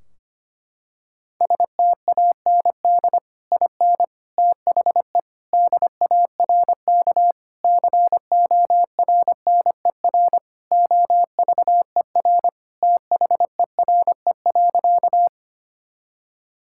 Morse code, heard as a keyed tone, about 25 wpm, 700 Hz. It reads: STAND IN THE DARK CORNER OVER THERE.
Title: Sentences from Top 1600 Words 25wpm